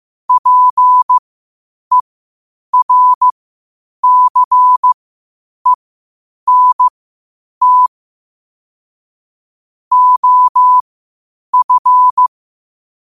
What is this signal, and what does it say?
Morse code, heard as a keyed tone, about 15 wpm, spaced as for 8 wpm, 1000 Hz. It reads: PERCENT OF
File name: qotdfiles / 8wpm